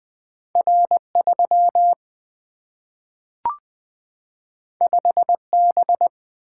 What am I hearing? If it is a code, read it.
Morse code, heard as a keyed tone, about 20 wpm, 700 Hz. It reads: R 3 5 B
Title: Single-Letter Single-Number 20wpm